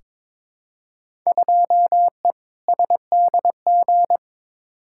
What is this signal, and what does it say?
Morse code, heard as a keyed tone, about 22 wpm, 700 Hz. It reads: 2E SDG
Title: Field Day Exchanges - 22wpm